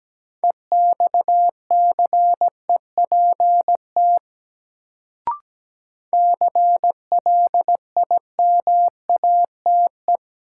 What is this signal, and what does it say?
Morse code, heard as a keyed tone, about 17 wpm, 700 Hz. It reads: EXCEPT CLIMATE
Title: Sentences from Top 1600 Words - New Word Drill - 17wpm